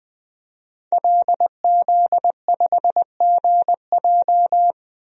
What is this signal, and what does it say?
Morse code, heard as a keyed tone, about 20 wpm, 700 Hz. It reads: LZ5GJ
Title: DXCC-100 Call Signs - 5 Characters - 20wpm